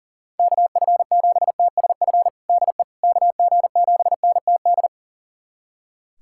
Morse code, heard as a keyed tone, about 37 wpm, 700 Hz. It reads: KF7TSF DE KG7NTD
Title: USA Callsign DE Callsign - 6 Characters - 40wpm